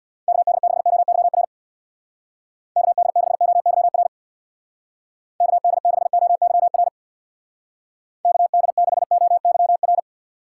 Morse code, heard as a keyed tone, about 63 wpm, 690 Hz. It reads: KD6OYR KD6OYR KD6OYR KD6OYR